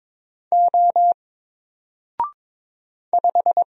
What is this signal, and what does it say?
Morse code, heard as a keyed tone, about 22 wpm, 700 Hz. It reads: O 5
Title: Single Letter-Number Alt 22wpm